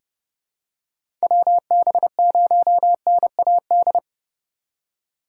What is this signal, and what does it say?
Morse code, heard as a keyed tone, about 30 wpm, 700 Hz. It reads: WB0NAD
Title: USA Call Signs - 6 Characters - 2x Character Spacing - 30wpm